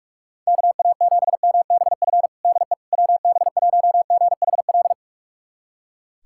Morse code, heard as a keyed tone, about 46 wpm, 700 Hz. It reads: KA7MBF DE WB1GHL
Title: USA Callsign DE Callsign - 6 Characters - 45wpm